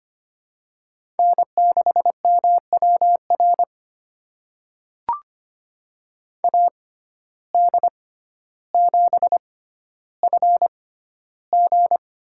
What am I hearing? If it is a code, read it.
Morse code, heard as a keyed tone, about 25 wpm, spaced as for 13 wpm, 700 Hz. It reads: N6MWR AD7FG
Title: USA Call Signs - 5 Characters - 6x Character Spacing - 25wpm